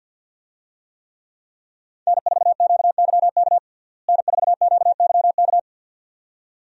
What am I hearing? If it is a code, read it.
Morse code, heard as a keyed tone, about 50 wpm, 700 Hz. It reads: N4QYK N4QYK
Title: USA Call Signs - 5 Characters - Repeated 2x - 50wpm